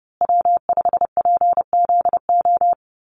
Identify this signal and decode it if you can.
Morse code, heard as a keyed tone, about 30 wpm, 700 Hz. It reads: W5PZO
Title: USA Call Signs Repeated 2x 30wpm